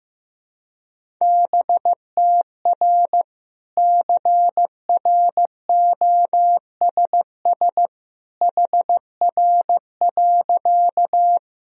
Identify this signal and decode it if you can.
Morse code, heard as a keyed tone, about 15 wpm, 700 Hz. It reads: BTR CROSS HR.
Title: Sentences from Top 400 Words and Abbreviations 15wpm